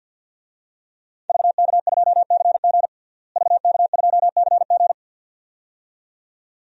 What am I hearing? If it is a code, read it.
Morse code, heard as a keyed tone, about 46 wpm, 700 Hz. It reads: VK2CG VK2CG
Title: DXCC-100 Call Signs - 5 Characters - Repeated 2x - 50wpm